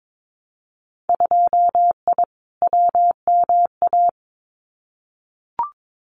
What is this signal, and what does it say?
Morse code, heard as a keyed tone, about 22 wpm, 700 Hz. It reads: 2I WMA
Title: Winter Field Day Exchanges Encore 22wpm